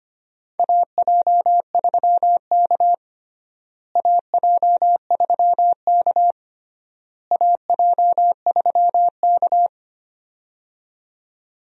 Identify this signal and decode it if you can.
Morse code, heard as a keyed tone, about 25 wpm, 700 Hz. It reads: AJ3K AJ3K AJ3K
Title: USA Call Signs - 4 Characters - Repeated 3x - 25wpm